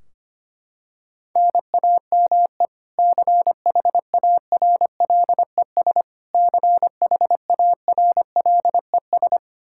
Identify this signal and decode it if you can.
Morse code, heard as a keyed tone, about 25 wpm, 700 Hz. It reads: NAME CHARLES CHARLES
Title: Practice QSO - Exchange 1 - Beginning - 25wpm